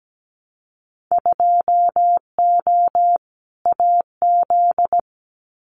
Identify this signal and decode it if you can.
Morse code, heard as a keyed tone, about 17 wpm, 690 Hz. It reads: 2O AZ
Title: Winter Field Day Exchanges 17wpm